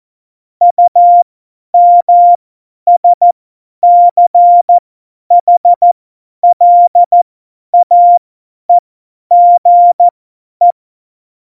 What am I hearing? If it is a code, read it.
Morse code, heard as a keyed tone, about 14 wpm, spaced as for 10 wpm, 700 Hz. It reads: UMSCHLAEGE
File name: Morsekurs / Witze / Witz 13. Februar 2018, 50 F70